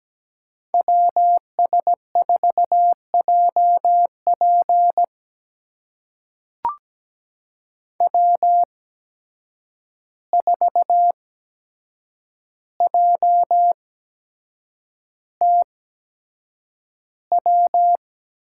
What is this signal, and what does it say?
Morse code, heard as a keyed tone, about 17 wpm, spaced as for 7 wpm, 700 Hz. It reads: WS4JP W4JTW
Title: USA Call Signs - 5 Characters - 8x Character Spacing - 17wpm